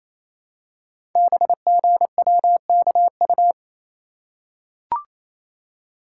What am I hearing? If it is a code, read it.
Morse code, heard as a keyed tone, about 27 wpm, 700 Hz. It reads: BGWKU
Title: Code Groups - 5 Letters - 28wpm